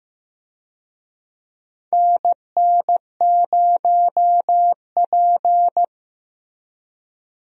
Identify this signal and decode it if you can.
Morse code, heard as a keyed tone, about 15 wpm, 700 Hz. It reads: NN0P
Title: USA Call Signs - 4 Characters - Repeated 3x - 15wpm